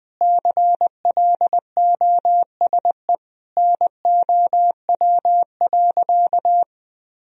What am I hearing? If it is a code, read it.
Morse code, heard as a keyed tone, about 20 wpm, 700 Hz. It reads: CLOSE NOW.
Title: Sentences from Top 1000 Words - Review Part 1 - 20wpm